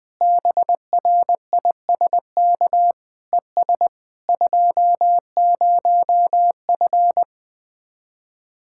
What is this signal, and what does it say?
Morse code, heard as a keyed tone, about 20 wpm, 700 Hz. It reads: BRISK ES 20F
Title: Speed-Racing Common QSO Phrases - 17wpm